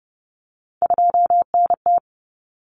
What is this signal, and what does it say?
Morse code, heard as a keyed tone, about 30 wpm, 700 Hz. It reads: 2NT
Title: Common Cut Numbers - 3-Digit 30wpm